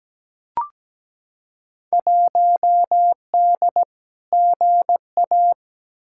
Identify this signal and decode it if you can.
Morse code, heard as a keyed tone, about 17 wpm, 700 Hz. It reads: 1D GA
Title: Field Day Exchanges - 17wpm